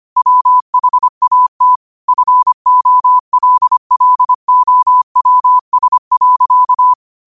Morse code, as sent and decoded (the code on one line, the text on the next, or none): .-- .... .- - / ..-. --- .-.. .-.. --- .-- ... .-.-.-
WHAT FOLLOWS.